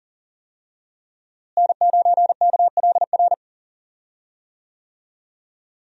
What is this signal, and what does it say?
Morse code, heard as a keyed tone, about 37 wpm, 700 Hz. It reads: N9KLR
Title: USA Call Signs - 5 Characters - Repeated 3x - 40wpm